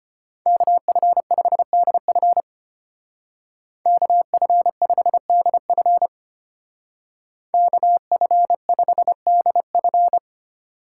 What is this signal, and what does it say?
Morse code, heard as a keyed tone, about 29 wpm, 700 Hz. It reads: KF5DF KF5DF KF5DF
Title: Speed Racing - USA 2x2 Call Signs 25wpm